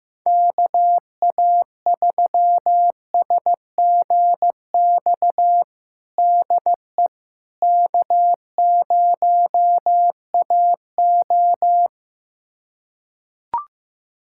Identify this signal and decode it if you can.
Morse code, heard as a keyed tone, about 15 wpm, 700 Hz. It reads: KA3SGX DE K0AO
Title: USA Callsign DE Callsign 15wpm